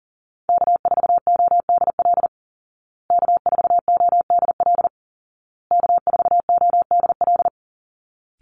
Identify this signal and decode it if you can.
Morse code, heard as a keyed tone, about 40 wpm, 700 Hz. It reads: K4ODL K4ODL K4ODL
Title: Call Signs Repeated 2x - 40wpm